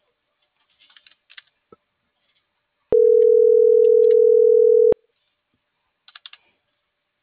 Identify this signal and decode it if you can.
Ringback tone (North America), 440 + 480 Hz.